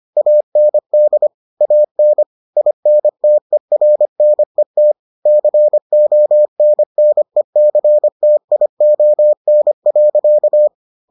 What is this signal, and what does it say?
Morse code, heard as a keyed tone, about 25 wpm, 600 Hz. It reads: AND AN INTERNET CONNECTION.